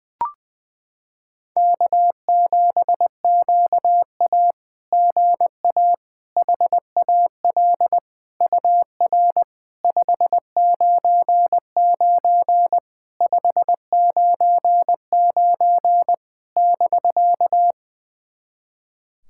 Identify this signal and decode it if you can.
Morse code, heard as a keyed tone, about 20 wpm, 700 Hz. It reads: K7QA GA HAL UR 599 599 <BK>
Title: SOTA QSO Practice 20wpm